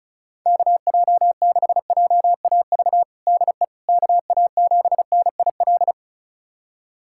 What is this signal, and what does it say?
Morse code, heard as a keyed tone, about 36 wpm, 700 Hz. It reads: KJ6JAV DE KA7NIL
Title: USA Callsign DE Callsign 35wpm